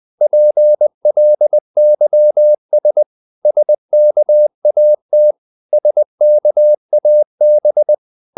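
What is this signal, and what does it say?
Morse code, heard as a keyed tone, about 20 wpm, 600 Hz. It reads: PLYS SKAT SKAB
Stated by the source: Morse code practice of short Danish words at 20 wpm, from CW med Gnister.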